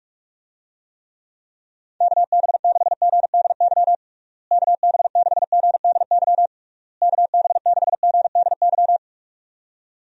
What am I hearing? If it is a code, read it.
Morse code, heard as a keyed tone, about 46 wpm, 700 Hz. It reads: KB6GDY KB6GDY KB6GDY
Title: USA Call Signs - 6 Characters - Repeated 3x - 45wpm